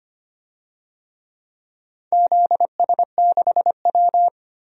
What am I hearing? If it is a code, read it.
Morse code, heard as a keyed tone, about 25 wpm, 700 Hz. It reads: ZS6W
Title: DXCC-100 Call Signs - 4 Characters - 25wpm